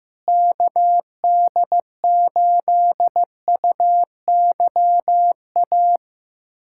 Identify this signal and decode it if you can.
Morse code, heard as a keyed tone, about 16 wpm, 700 Hz. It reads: KD8UYA
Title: USA Callsign DE Callsign - 6 Characters - 15wpm